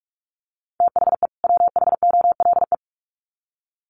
Morse code, heard as a keyed tone, about 44 wpm, 700 Hz. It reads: THE WHOLE
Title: Sets of 2 Words 45wpm